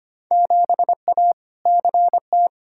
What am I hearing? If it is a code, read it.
Morse code, heard as a keyed tone, about 25 wpm, 700 Hz. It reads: 7A CT
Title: Field Day Exchanges - 25wpm